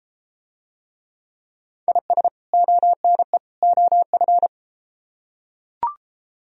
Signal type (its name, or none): Morse code